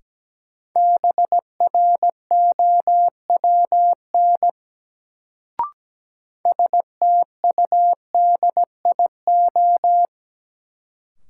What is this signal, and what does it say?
Morse code, heard as a keyed tone, about 17 wpm, 700 Hz. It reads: BROWN STUDIO
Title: Sentences from Top 1200 Words - New Word Drill Alt - 17wpm